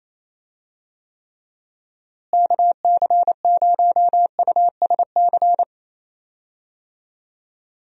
Morse code, heard as a keyed tone, about 28 wpm, 700 Hz. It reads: KC0USC
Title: USA Call Signs - 6 Characters - Repeated 3x - 28wpm